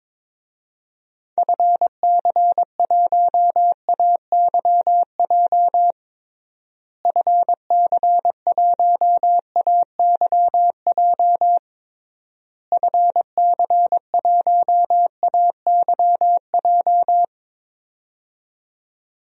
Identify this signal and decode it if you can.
Morse code, heard as a keyed tone, about 22 wpm, 700 Hz. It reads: FC1AYJ FC1AYJ FC1AYJ
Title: DXCC-100 Call Signs - 6 Characters - Repeated 3x - 22wpm